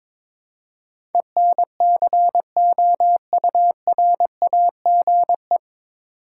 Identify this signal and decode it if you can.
Morse code, heard as a keyed tone, about 22 wpm, 700 Hz. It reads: ENCOURAGE
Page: Morse Code Ninja practice 22wpm